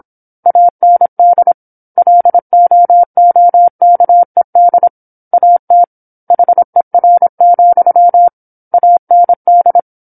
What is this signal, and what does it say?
Morse code, heard as a keyed tone, about 27 wpm, 700 Hz. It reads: AND LOOKED AT HER, AND
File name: 08-Chapter-08 26 WPM